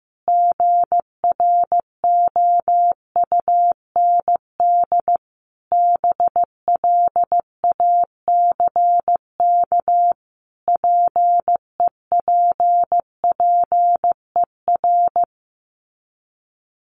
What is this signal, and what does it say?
Morse code, heard as a keyed tone, about 15 wpm, 700 Hz. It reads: GROUND BLACK PEPPER
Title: Sets of 5 Words Encore 15wpm